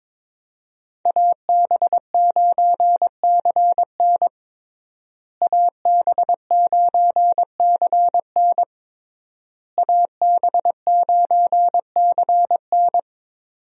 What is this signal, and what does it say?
Morse code, heard as a keyed tone, about 21 wpm, 700 Hz. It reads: AB9CN AB9CN AB9CN
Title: USA Call Signs - 5 Characters - Repeated 3x - 22wpm